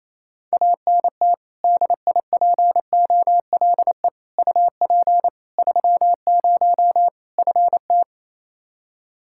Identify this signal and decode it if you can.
Morse code, heard as a keyed tone, about 28 wpm, 700 Hz. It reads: ANT DIPOLE UP 30 FT
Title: K1USN SST QSO Practice 28wpm